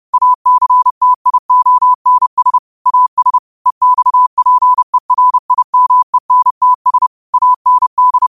Morse code, heard as a keyed tone, about 30 wpm, 1000 Hz. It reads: ACTIONS AS EXPERIMENTS AND
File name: qotdfiles / 30wpm